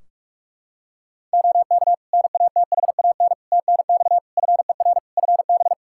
Morse code, heard as a keyed tone, about 46 wpm, 700 Hz. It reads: OK NATHAN TNX FER FB
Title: Practice QSO - Exchange 4 - Ending - 45wpm